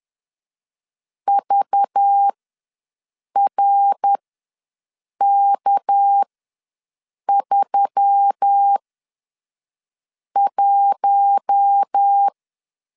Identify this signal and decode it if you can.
Morse code, heard as a keyed tone, about 10 wpm, spaced as for 5 wpm, 800 Hz. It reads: VRK31
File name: LEZIONI / 6 PKX0 / 6 PK ws 10wpm 3745